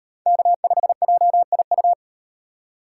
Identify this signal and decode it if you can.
Morse code, heard as a keyed tone, about 38 wpm, 700 Hz. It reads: K5JIU